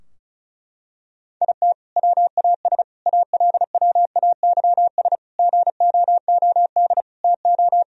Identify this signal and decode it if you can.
Morse code, heard as a keyed tone, about 34 wpm, 700 Hz. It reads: IT WAS ALWAYS GOOD TO
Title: Sentences from Top 1000 Words - Review Part 4 - 35wpm